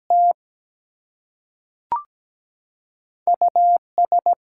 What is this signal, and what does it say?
Morse code, heard as a keyed tone, about 17 wpm, spaced as for 5 wpm, 700 Hz. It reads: TUS
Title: Two Letter Words - 17wpm